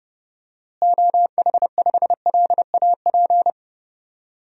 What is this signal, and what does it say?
Morse code, heard as a keyed tone, about 30 wpm, 700 Hz. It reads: OH5LAP
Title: DXCC-100 Call Signs - 6 Characters - Repeated 3x - 30wpm